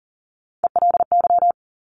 Morse code, heard as a keyed tone, about 38 wpm, 700 Hz. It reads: ELY
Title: Top Three Letter Combinations in Words 40wpm